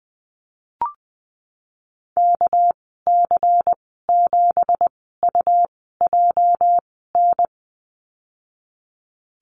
Morse code, heard as a keyed tone, about 20 wpm, spaced as for 15 wpm, 700 Hz. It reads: KC7UJN
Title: 207 Lesson 33g - Letter j - Call Signs 20wpm